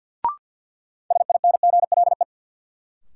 Morse code, heard as a keyed tone, about 53 wpm, 700 Hz. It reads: SINGLE
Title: Sentences from Top 700 Words - New Word Drill - 50wpm